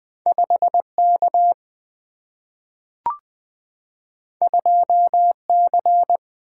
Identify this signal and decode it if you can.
Morse code, heard as a keyed tone, about 20 wpm, 700 Hz. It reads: 5 K 2 C
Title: Single-Letter Single-Number 20wpm